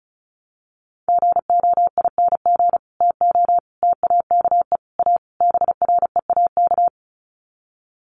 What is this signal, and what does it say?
Morse code, heard as a keyed tone, about 34 wpm, 700 Hz. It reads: GOING TO TAKE A BREAK
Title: Sets of 5 Words 35wpm